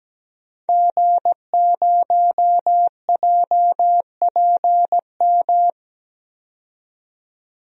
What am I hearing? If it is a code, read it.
Morse code, heard as a keyed tone, about 17 wpm, 700 Hz. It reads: G0JPM
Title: DXCC-100 Call Signs - 5 Characters - 17wpm